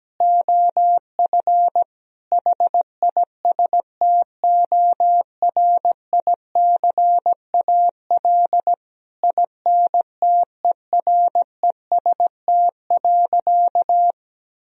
Morse code, heard as a keyed tone, about 17 wpm, 700 Hz. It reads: OF HISTORICAL INTEREST.
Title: Sentences from Top 1000 Words - Review Part 4 - 17wpm